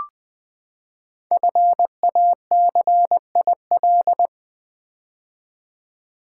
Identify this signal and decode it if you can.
Morse code, heard as a keyed tone, about 20 wpm, 700 Hz. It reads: FACIL
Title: Build A Word - 5 Letters At A Time - 10-Letter Words - 20wpm